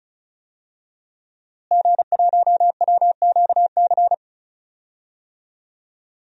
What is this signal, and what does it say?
Morse code, heard as a keyed tone, about 36 wpm, 700 Hz. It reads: G1WQC
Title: DXCC-100 Call Signs - 5 Characters - Repeated 2x - 35wpm